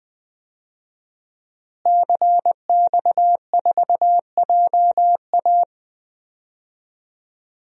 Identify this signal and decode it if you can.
Morse code, heard as a keyed tone, about 20 wpm, 700 Hz. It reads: CX4JA
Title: DXCC-100 Call Signs - 5 Characters - Repeated 2x - 20wpm